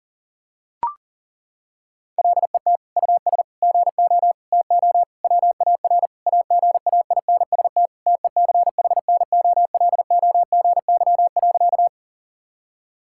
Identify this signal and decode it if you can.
Morse code, heard as a keyed tone, about 40 wpm, 700 Hz. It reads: LET US GO TO WAR AGAINST TECHNOLOGY.
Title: Sentences from Top 1000 Words - Review Part 5 - 40wpm